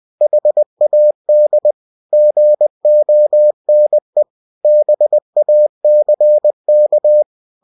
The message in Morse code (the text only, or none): HAD GONE BACK